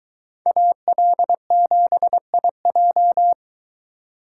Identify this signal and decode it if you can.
Morse code, heard as a keyed tone, about 23 wpm, 700 Hz. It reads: AL7IJ